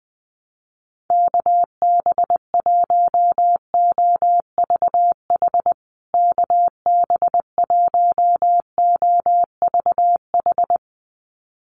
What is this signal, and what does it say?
Morse code, heard as a keyed tone, about 20 wpm, 700 Hz. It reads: KB1OVH KB1OVH
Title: USA Call Signs Repeated 2x 20wpm